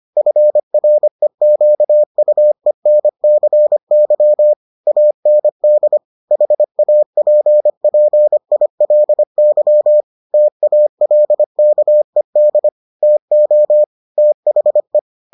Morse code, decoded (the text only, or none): FREQUENCY AND HAPPILY TALKED TO THE